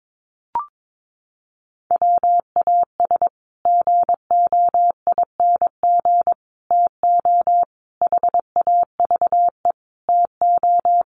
Morse code, heard as a keyed tone, about 22 wpm, 700 Hz. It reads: WAS GOING TO HAVE TO
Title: Sets of 5 Words Encore 22wpm